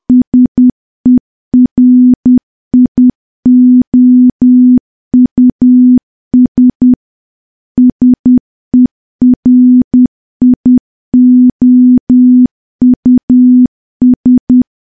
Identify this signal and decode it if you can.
Morse code, heard as a keyed tone, about 10 wpm, 250 Hz. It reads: SERIOUS SERIOUS
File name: morse / morse-training-10WPM